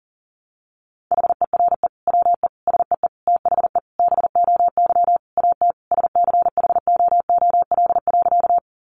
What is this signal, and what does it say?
Morse code, heard as a keyed tone, about 40 wpm, 700 Hz. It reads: HERE WE SEE THE BOY AT SCHOOL.
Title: Sentences from Top 300 Words 40wpm